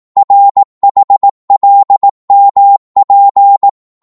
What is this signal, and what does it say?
Morse code, heard as a keyed tone, about 18 wpm, 800 Hz. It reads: RHLMP